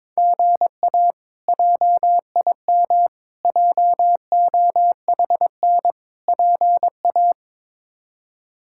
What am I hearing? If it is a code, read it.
Morse code, heard as a keyed tone, about 22 wpm, 700 Hz. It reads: GA JIM JOHN PA
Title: K1USN SST QSO Practice - 22wpm